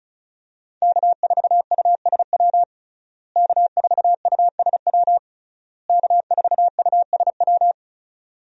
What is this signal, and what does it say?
Morse code, heard as a keyed tone, about 36 wpm, 700 Hz. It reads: K4USW K4USW K4USW
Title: USA Call Signs - 5 Characters - Repeated 3x - 35wpm